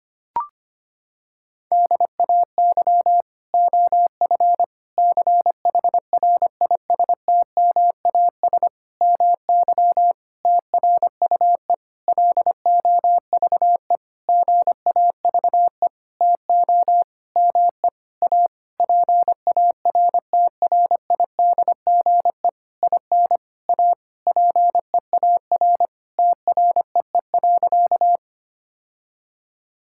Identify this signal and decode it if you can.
Morse code, heard as a keyed tone, about 25 wpm, 700 Hz. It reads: DAY OF CHRISTMAS MY TRUE LOVE GAVE TO ME A PARTRIDGE IN A PEAR TREE.
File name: practice / 25wpm / Sentences with 9-Word Phrases - 25wpm